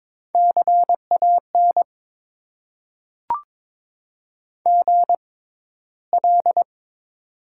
Morse code, heard as a keyed tone, about 22 wpm, spaced as for 13 wpm, 700 Hz. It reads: CAN GL
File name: Top 90 QSO Elements - 6x Character Spacing - 22wpm